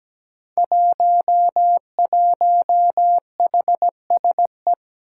Morse code, heard as a keyed tone, about 17 wpm, 700 Hz. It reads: 11HSE
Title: Five Challenging Letter-Number Combinations 17wpm